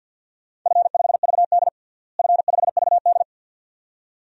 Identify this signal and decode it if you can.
Morse code, heard as a keyed tone, about 53 wpm, 700 Hz. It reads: F5VD F5VD
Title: DXCC-100 Call Signs - 4 Characters - Repeated 2x - 50wpm